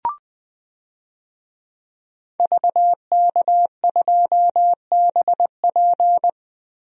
Morse code, heard as a keyed tone, about 20 wpm, 700 Hz. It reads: VK2BP
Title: DXCC-100 Call Signs - 5 Characters - Repeated 2x - 20wpm